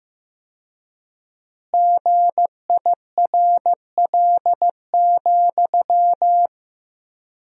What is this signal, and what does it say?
Morse code, heard as a keyed tone, about 15 wpm, 700 Hz. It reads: GIRL,
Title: 4x Word-Spacing - Sentences from Top 400 Words - 15wpm